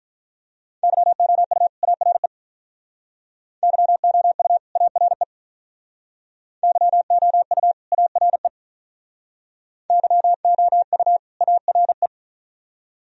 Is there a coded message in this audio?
Yes: Morse code from 0.8 to 12.1 s, heard as a keyed tone, about 45 wpm, 700 Hz. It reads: YOU ARE YOU ARE YOU ARE YOU ARE